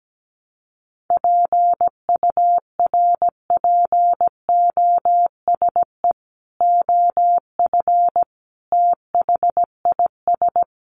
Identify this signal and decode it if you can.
Morse code, heard as a keyed tone, about 17 wpm, 700 Hz. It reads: PURPOSE OF THIS
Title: Sets of 5 Words 17wpm